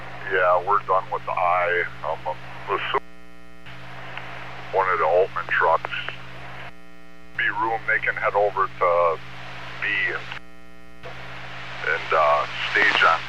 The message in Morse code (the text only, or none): none